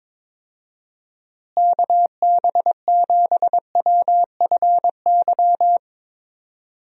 Morse code, heard as a keyed tone, about 22 wpm, 700 Hz. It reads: KB7WFY
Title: USA Call Signs - 6 Characters - Repeated 2x - 22wpm